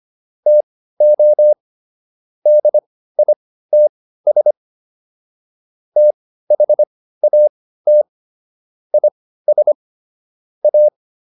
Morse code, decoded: TO DITS THAT IS A

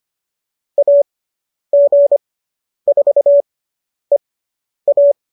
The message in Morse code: AG4EA

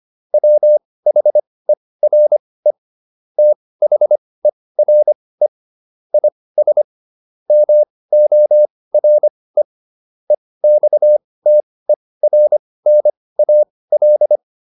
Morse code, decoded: WHERE THERE IS MORE EXTERNAL

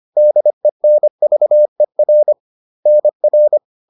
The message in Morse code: DENVER NR

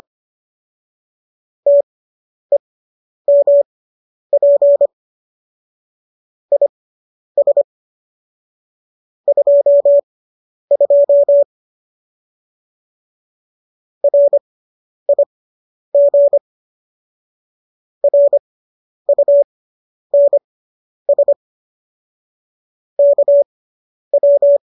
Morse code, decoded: TEMP IS 22 RIG RUNS KW

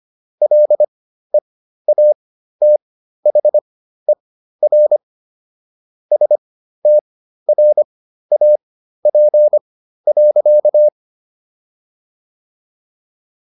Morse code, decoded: LEATHER STRAP.

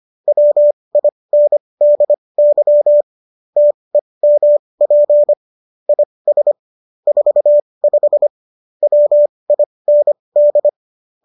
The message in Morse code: WINDY TEMP IS 45 WIND

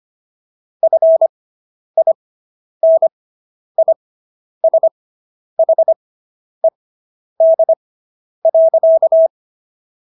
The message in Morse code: FINISHED.

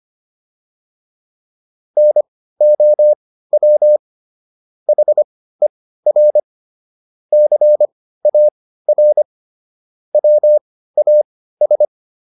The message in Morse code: NOW HER CAR WAS

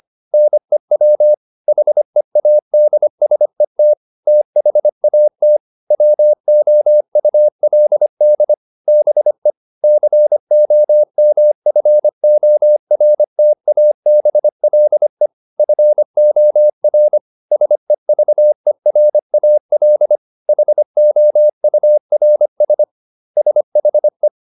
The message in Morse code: NEW HEADSET THAT WOULD BE COMFORTABLE FOR SEVERAL HOURS SHE

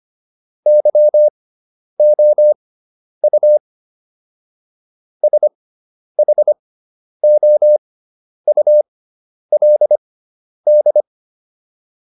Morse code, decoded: YOU SHOULD

